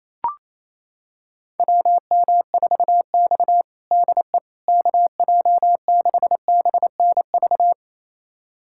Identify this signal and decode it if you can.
Morse code, heard as a keyed tone, about 27 wpm, 700 Hz. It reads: WM4X DE KJ6BNV
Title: USA Callsign DE Callsign 28wpm